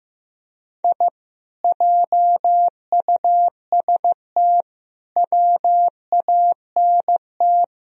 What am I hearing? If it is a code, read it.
Morse code, heard as a keyed tone, about 17 wpm, 700 Hz. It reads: I JUST WANT
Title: Speed-Racing - Sets of 4 Words 15wpm